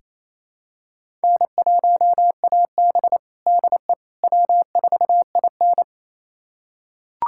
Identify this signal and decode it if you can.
Morse code, heard as a keyed tone, about 28 wpm, 700 Hz. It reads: N1AB DE W4IN
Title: USA Callsign DE Callsign - 4 Characters - 28wpm